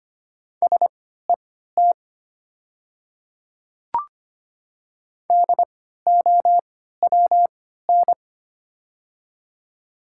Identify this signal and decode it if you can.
Morse code, heard as a keyed tone, about 25 wpm, spaced as for 14 wpm, 700 Hz. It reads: SET DOWN
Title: 3x Character Spacing - Top 200 Words - 25wpm